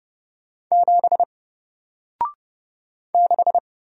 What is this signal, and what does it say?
Morse code, heard as a keyed tone, about 30 wpm, 700 Hz. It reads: 7 6
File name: Single-Letter-Number-30wpm